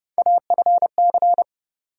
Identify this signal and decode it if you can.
Morse code, heard as a keyed tone, about 30 wpm, 700 Hz. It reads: AFC
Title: USA Call Sign Postfixes - 3 Letters 30wpm